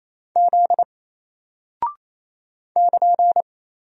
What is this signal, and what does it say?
Morse code, heard as a keyed tone, about 27 wpm, 700 Hz. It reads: Z <KN>
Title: Single Character (Letter:Number:Punctuation:Prosign) Alt 28wpm